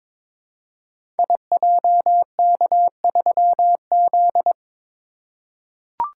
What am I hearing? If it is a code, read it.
Morse code, heard as a keyed tone, about 21 wpm, 700 Hz. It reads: IJK3Z
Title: Code Groups - 5 Letter:Numbers - 22wpm